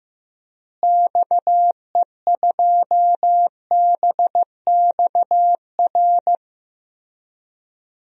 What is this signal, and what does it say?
Morse code, heard as a keyed tone, about 15 wpm, 700 Hz. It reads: XE2BXR
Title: DXCC-100 Call Signs - 6 Characters - Repeated 3x - 15wpm